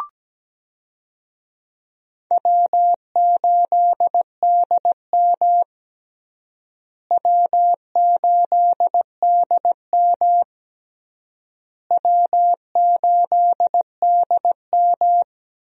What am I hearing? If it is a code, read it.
Morse code, heard as a keyed tone, about 17 wpm, 700 Hz. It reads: W8DM W8DM W8DM
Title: USA Call Signs - 4 Characters - Repeated 3x - 17wpm